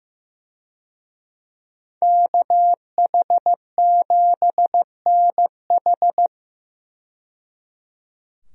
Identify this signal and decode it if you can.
Morse code, heard as a keyed tone, about 15 wpm, 700 Hz. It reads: KH7NH
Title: USA Call Signs - 5 Characters - 15wpm